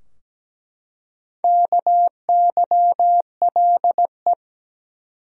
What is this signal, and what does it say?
Morse code, heard as a keyed tone, about 17 wpm, 700 Hz. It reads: KYLE
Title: USA Common Names - 17wpm